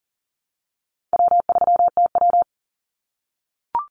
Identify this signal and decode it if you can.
Morse code, heard as a keyed tone, about 40 wpm, 700 Hz. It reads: W3TW
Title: Easy Call Signs - 40wpm